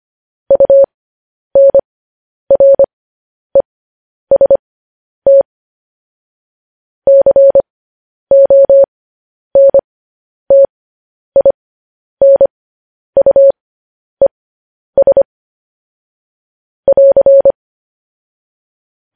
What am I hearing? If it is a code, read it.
Morse code, heard as a keyed tone, about 25 wpm, spaced as for 10 wpm, 550 Hz. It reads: UNREST CONTINUES <AR>
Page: News Headlines in Morse Code at 10 WPM